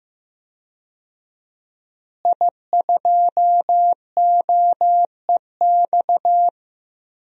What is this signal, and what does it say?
Morse code, heard as a keyed tone, about 16 wpm, 700 Hz. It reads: I2OEX
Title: DXCC-100 Call Signs - 5 Characters - Repeated 2x - 15wpm